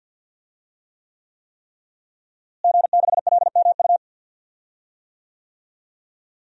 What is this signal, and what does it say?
Morse code, heard as a keyed tone, about 51 wpm, 700 Hz. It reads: G6LMU
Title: DXCC-100 Call Signs - 5 Characters - Repeated 3x - 50wpm